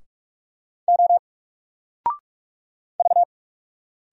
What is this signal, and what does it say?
Morse code, heard as a keyed tone, about 45 wpm, 700 Hz. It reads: O V
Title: Single Letter-Number 45wpm